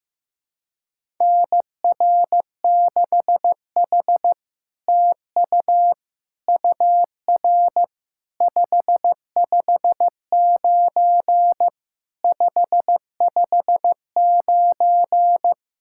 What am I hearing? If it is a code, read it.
Morse code, heard as a keyed tone, about 15 wpm, 700 Hz. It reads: NR6H TU UR 559 559
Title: SOTA QSO Practice 15wpm